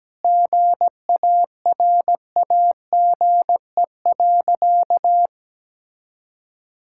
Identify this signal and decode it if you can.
Morse code, heard as a keyed tone, about 17 wpm, 700 Hz. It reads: GARAGE.
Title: Sentences from Top 1600 Words 17wpm